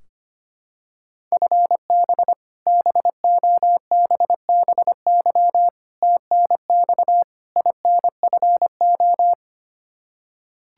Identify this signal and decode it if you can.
Morse code, heard as a keyed tone, about 25 wpm, 700 Hz. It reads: FB BOBBY TNX INFO
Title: Practice QSO - Exchange 3 - More Info - 25wpm